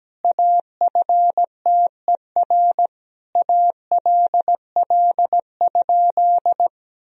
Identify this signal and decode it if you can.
Morse code, heard as a keyed tone, about 17 wpm, 700 Hz. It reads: AFTER ALL?
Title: Sentences from Top 1000 Words - Review Part 2 - 17wpm